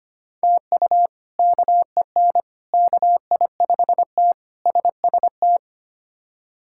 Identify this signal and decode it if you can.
Morse code, heard as a keyed tone, about 25 wpm, 700 Hz. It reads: TU KEN KI5T SST
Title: K1USN SST QSO Practice - 25wpm